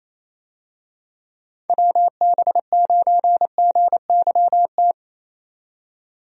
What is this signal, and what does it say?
Morse code, heard as a keyed tone, about 28 wpm, 700 Hz. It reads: WB9GYT
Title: USA Call Signs - 6 Characters - 28wpm